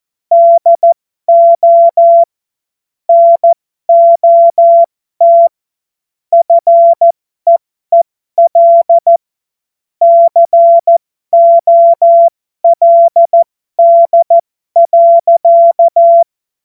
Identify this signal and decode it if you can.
Morse code, heard as a keyed tone, about 14 wpm, spaced as for 12 wpm, 680 Hz. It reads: DO NOT FEEL COLD.
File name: Morsekurs / Witze